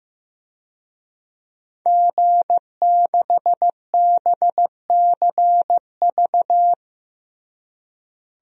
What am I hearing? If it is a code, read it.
Morse code, heard as a keyed tone, about 15 wpm, 700 Hz. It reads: G6BCV
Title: DXCC-100 Call Signs - 5 Characters - Repeated 3x - 15wpm